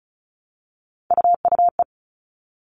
Morse code, heard as a keyed tone, about 34 wpm, 700 Hz. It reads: UUE
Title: All Cut Numbers - 3-Digit 35wpm